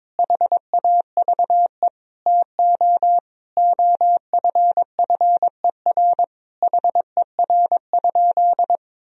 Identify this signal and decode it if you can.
Morse code, heard as a keyed tone, about 22 wpm, 700 Hz. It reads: HAVE TO OFFER HER?
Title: Sentences from Top 1000 Words - Review Part 4 - 22wpm